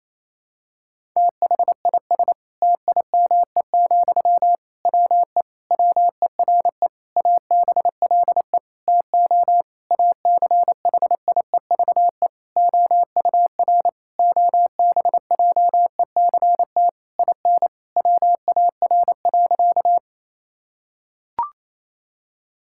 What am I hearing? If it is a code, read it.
Morse code, heard as a keyed tone, about 28 wpm, 700 Hz. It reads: THIS TIME, WE WERE ABLE TO ACHIEVE OUR OBJECT IN WAR.
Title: Sentences from Top 1000 Words - Review Part 6 - 28wpm